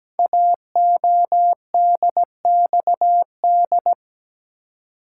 Morse code, heard as a keyed tone, about 17 wpm, 700 Hz. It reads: AODXD
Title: Code Groups - 5 Letters - 17wpm